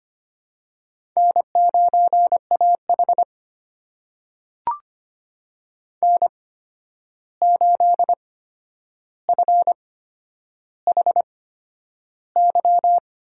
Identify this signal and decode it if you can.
Morse code, heard as a keyed tone, about 25 wpm, spaced as for 10 wpm, 700 Hz. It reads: N9AH N8FHY
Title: USA Call Signs - 8x Character Spacing - 25wpm